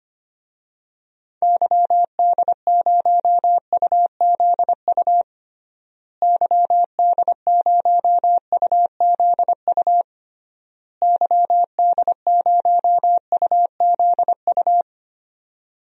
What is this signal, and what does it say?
Morse code, heard as a keyed tone, about 25 wpm, 700 Hz. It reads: YD0UZU YD0UZU YD0UZU
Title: DXCC-100 Call Signs - 6 Characters - Repeated 3x - 25wpm